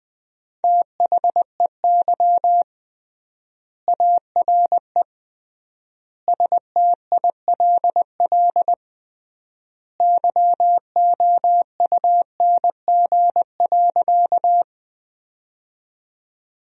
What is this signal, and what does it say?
Morse code, heard as a keyed tone, about 20 wpm, 700 Hz. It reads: THEY ARE STILL YOUNG.
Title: 3x Word-Spacing - Sentences from Top 400 Words - 20wpm